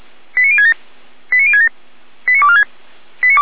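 Five-tone selective call (ZVEI), heard as quick runs of five tones, tones of about 70 ms. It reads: 89067 89067 89246
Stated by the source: ZVEI five-tone call received on an SDR.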